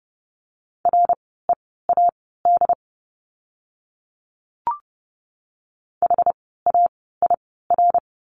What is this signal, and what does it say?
Morse code, heard as a keyed tone, about 30 wpm, spaced as for 17 wpm, 700 Hz. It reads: READ HAIR
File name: Lesson 17f - Letter u - Words 30wpm